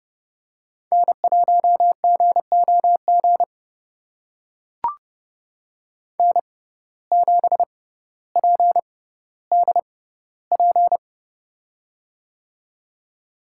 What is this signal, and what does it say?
Morse code, heard as a keyed tone, about 30 wpm, spaced as for 15 wpm, 700 Hz. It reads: N1GOG N7PDP
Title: USA Call Signs - 5 Characters - 6x Character Spacing - 30wpm